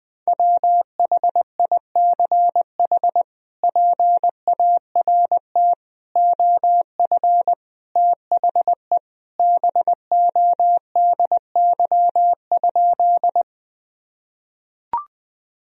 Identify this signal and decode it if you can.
Morse code, heard as a keyed tone, about 20 wpm, 700 Hz. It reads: WHICH PART OF THE BODY?
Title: Sentences from Top 1000 Words - Review Part 2 - 20wpm